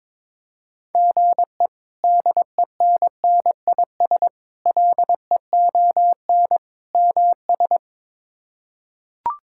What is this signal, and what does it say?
Morse code, heard as a keyed tone, about 22 wpm, 700 Hz. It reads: GE DENNIS LEON MS
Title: K1USN SST QSO Practice - 22wpm